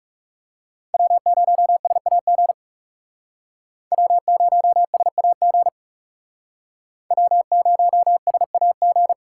Morse code, heard as a keyed tone, about 38 wpm, 700 Hz. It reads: W0SAG W0SAG W0SAG